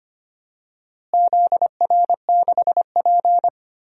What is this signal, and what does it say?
Morse code, heard as a keyed tone, about 25 wpm, 700 Hz. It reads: ZR6P
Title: DXCC-100 Call Signs - 4 Characters - Repeated 3x - 25wpm